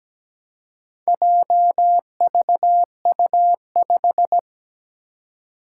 Morse code, heard as a keyed tone, about 17 wpm, 700 Hz. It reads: JVU5
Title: Four Challenging Letter-Number Combinations 17wpm